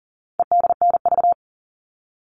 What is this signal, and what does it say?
Morse code, heard as a keyed tone, about 40 wpm, 700 Hz. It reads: EDNV
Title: All Cut Numbers - 4-Digit 40wpm